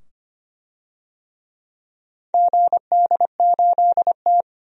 Morse code, heard as a keyed tone, about 25 wpm, 700 Hz. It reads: GD8T
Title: DXCC-100 Call Signs - 4 Characters - 25wpm